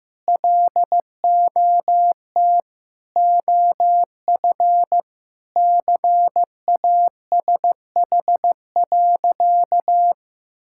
Morse code, heard as a keyed tone, about 15 wpm, 700 Hz. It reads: LOT OF CASH.
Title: Sentences from Top 1000 Words - Review Part 5 - 15wpm